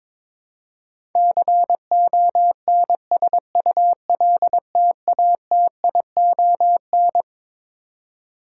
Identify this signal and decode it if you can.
Morse code, heard as a keyed tone, about 22 wpm, 700 Hz. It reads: CONSULTATION
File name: Twelve Letter Words - 22wpm